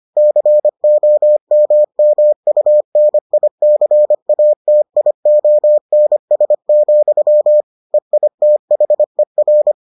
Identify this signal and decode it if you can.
Morse code, heard as a keyed tone, about 25 wpm, 600 Hz. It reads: COMMUNICATIONS, EITHER